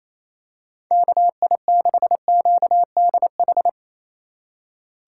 Morse code, heard as a keyed tone, about 27 wpm, 700 Hz. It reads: KI6QDH
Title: K1USN SST QSO Practice - 28wpm